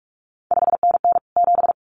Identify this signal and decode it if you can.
Morse code, heard as a keyed tone, about 46 wpm, 700 Hz. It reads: 5NN 7
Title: CQ WW DX Contest Exchange Encore - 45wpm